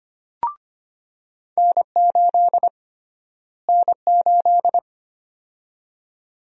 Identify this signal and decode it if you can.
Morse code, heard as a keyed tone, about 25 wpm, 700 Hz. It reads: N8 N8
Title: USA Call Sign Prefixes - Repeated 2x - 25wpm